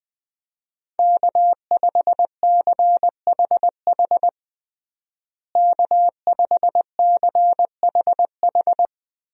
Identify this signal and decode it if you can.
Morse code, heard as a keyed tone, about 20 wpm, 700 Hz. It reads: K5CHH K5CHH
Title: USA Call Signs - 5 Characters - Repeated 3x - 20wpm